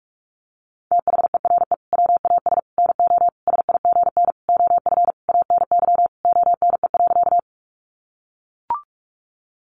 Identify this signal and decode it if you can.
Morse code, heard as a keyed tone, about 44 wpm, 700 Hz. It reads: THERE WAS NO SIGN OF ANY ONE.
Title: Sentences from Top 1000 Words 45wpm